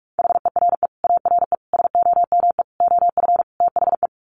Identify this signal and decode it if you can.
Morse code, heard as a keyed tone, about 44 wpm, 700 Hz. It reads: HERE ARE SOME OF THE
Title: Sets of 5 Words Encore 45wpm